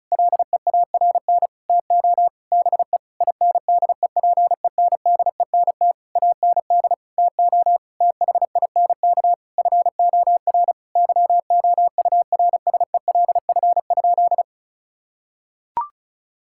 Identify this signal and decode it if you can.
Morse code, heard as a keyed tone, about 36 wpm, 700 Hz. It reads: LEARN TO BE INDEPENDENT AND TO THINK FOR YOURSELF?
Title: Sentences from Top 1000 Words - Review Part 6 - 35wpm